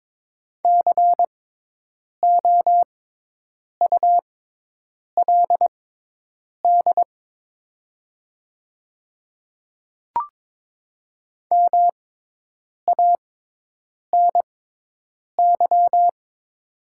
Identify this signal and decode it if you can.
Morse code, heard as a keyed tone, about 22 wpm, spaced as for 8 wpm, 700 Hz. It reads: COULD MANY
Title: 6x Character Spacing - Top 100 Words - 22wpm